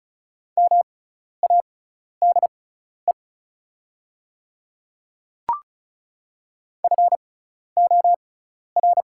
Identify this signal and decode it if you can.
Morse code, heard as a keyed tone, about 34 wpm, spaced as for 12 wpm, 700 Hz. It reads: MADE FOR
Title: 6x Character Spacing - Top 200 Words - 35wpm